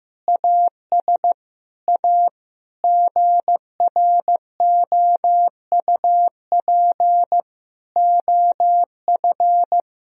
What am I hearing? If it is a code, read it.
Morse code, heard as a keyed tone, about 15 wpm, 700 Hz. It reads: AS A GROUP OF
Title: Sentences from Top 1000 Words - Review Part 1 - 15wpm